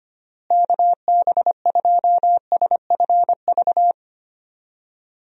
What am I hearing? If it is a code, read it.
Morse code, heard as a keyed tone, about 25 wpm, 700 Hz. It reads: KB2SFV
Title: USA Call Signs - 6 Characters - Repeated 2x - 25wpm